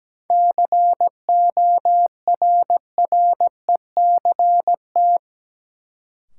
Morse code, heard as a keyed tone, about 17 wpm, 700 Hz. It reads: CORRECT
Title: Sentences from Top 700 Words - New Word Drill - 17wpm